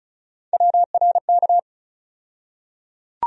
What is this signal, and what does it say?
Morse code, heard as a keyed tone, about 34 wpm, 700 Hz. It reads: WRK
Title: USA Call Sign Postfixes - 3 Letters 35wpm